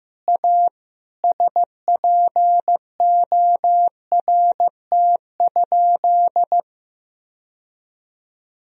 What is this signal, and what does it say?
Morse code, heard as a keyed tone, about 15 wpm, 700 Hz. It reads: A SPORT?
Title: Sentences from Top 1000 Words - Review Part 4 - 15wpm